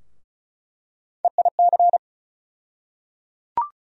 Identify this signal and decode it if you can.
Morse code, heard as a keyed tone, about 34 wpm, 700 Hz. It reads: EIC
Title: USA Call Sign Postfixes - 3 Letters 35wpm